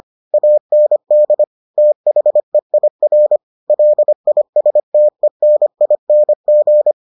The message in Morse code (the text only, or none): AND THEIR LISTENING